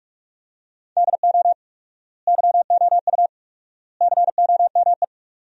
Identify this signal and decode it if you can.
Morse code, heard as a keyed tone, about 41 wpm, 700 Hz. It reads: DO YOU COME
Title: 4x Word-Spacing - Sentences from Top 200 Words - 45wpm